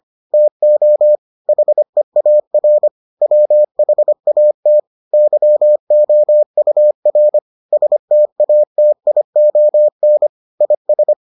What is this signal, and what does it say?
Morse code, heard as a keyed tone, about 25 wpm, 600 Hz. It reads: TO HEAR WHAT YOUR STATION IS